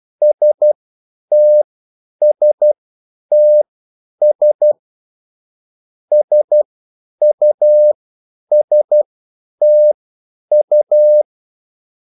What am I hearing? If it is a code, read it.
Morse code, heard as a keyed tone, about 12 wpm, spaced as for 9 wpm, 600 Hz. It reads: STSTS SUSTU
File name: QSL / CW lesson